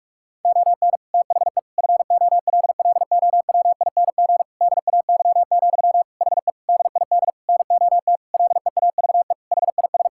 Morse code, heard as a keyed tone, about 41 wpm, 700 Hz. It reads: ON THE FOLLOWING DAY, HE DID NOT LEAVE HIS
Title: Sentences from Top 1000 Words - Review Part 6 - 45wpm